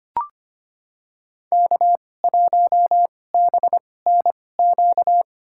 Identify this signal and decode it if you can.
Morse code, heard as a keyed tone, about 25 wpm, spaced as for 18 wpm, 700 Hz. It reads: K1BNQ
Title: USA Call Signs - 5 Characters - 2x Character Spacing - 25wpm